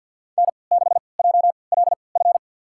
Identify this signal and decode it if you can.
Morse code, heard as a keyed tone, about 49 wpm, spaced as for 28 wpm, 700 Hz. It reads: N6JLF
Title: USA Call Signs - 5 Characters - 3x Character Spacing - 50wpm